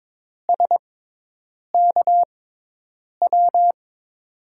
Morse code, heard as a keyed tone, about 22 wpm, spaced as for 8 wpm, 700 Hz. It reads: SKW
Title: USA Call Signs - 6 Characters - 6x Character Spacing - 22wpm